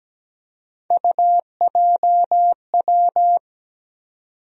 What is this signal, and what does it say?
Morse code, heard as a keyed tone, about 17 wpm, 700 Hz. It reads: UJW
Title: USA Call Sign Postfixes - 3 Letters 17wpm